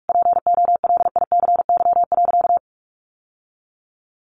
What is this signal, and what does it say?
Morse code, heard as a keyed tone, about 44 wpm, 700 Hz. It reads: POLICY.
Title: Sentences from Top 1200 Words 45wpm